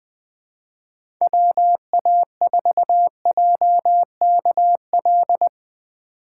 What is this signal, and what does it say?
Morse code, heard as a keyed tone, about 20 wpm, 700 Hz. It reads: WA4JKL
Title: K1USN SST QSO Practice - 20wpm